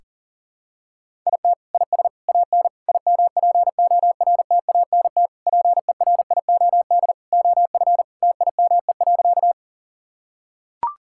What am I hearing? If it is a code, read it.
Morse code, heard as a keyed tone, about 40 wpm, 700 Hz. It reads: IT IS AN IMPORTANT PERIOD OF TIME.
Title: Sentences from Top 1000 Words - Review Part 5 - 40wpm